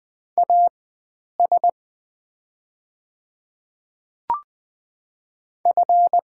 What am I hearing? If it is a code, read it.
Morse code, heard as a keyed tone, about 21 wpm, spaced as for 9 wpm, 690 Hz. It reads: AS F
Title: 4x Character Spacing - Top 100 Words - 20wpm